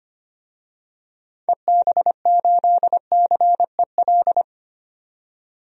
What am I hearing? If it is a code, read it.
Morse code, heard as a keyed tone, about 25 wpm, 700 Hz. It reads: EB8CEL
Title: DXCC-100 Call Signs - 6 Characters - 25wpm